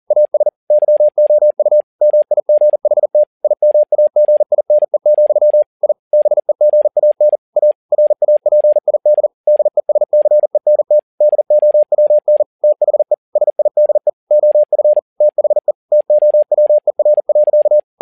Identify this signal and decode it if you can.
Morse code, heard as a keyed tone, about 40 wpm, 600 Hz. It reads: AS YOU MIGHT IMAGINE, I BEGAN A RAPID DESCENT DOWN THE SIDE OF THE TOWER.
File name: learning center / tower trouble / tower touble 40wpm